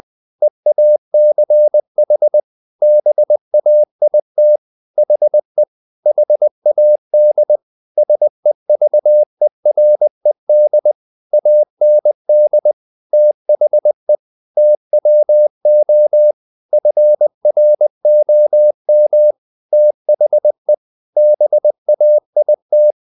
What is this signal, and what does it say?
Morse code, heard as a keyed tone, about 20 wpm, 600 Hz. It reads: EACH BAIT HE HAD SEVERED AND THE TWO FROM THE BAIT
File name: the old man and the sea / the old man and the sea part 1 20wpm